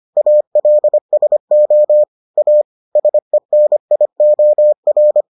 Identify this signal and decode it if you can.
Morse code, heard as a keyed tone, about 25 wpm, 600 Hz. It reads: ALSO A SENIOR